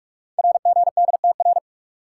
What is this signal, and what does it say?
Morse code, heard as a keyed tone, about 44 wpm, 690 Hz. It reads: RGDTR
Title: Code Groups - 5 Letters - 45wpm